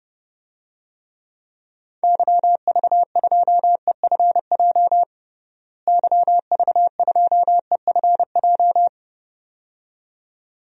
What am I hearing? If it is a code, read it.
Morse code, heard as a keyed tone, about 30 wpm, 700 Hz. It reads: YV2EFJ YV2EFJ
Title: DXCC-100 Call Signs - 6 Characters - Repeated 2x - 30wpm